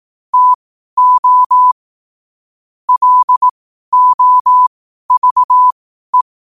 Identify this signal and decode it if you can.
Morse code, heard as a keyed tone, about 18 wpm, spaced as for 12 wpm, 1000 Hz. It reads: TO LOVE
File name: qotdfiles / 12wpm